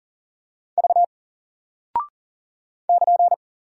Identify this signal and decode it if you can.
Morse code, heard as a keyed tone, about 40 wpm, 700 Hz. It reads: V <KN>
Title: Single Character (Letter:Number:Punctuation:Prosign) 40wpm